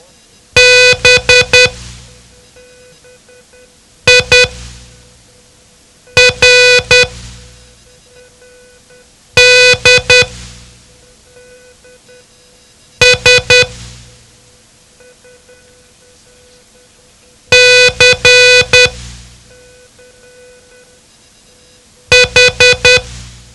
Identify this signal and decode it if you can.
Morse code, heard as a keyed tone, about 10 wpm, spaced as for 3 wpm, 500 Hz. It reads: BIRDS CH
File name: ham radio / CodeQuick / Code Quick 07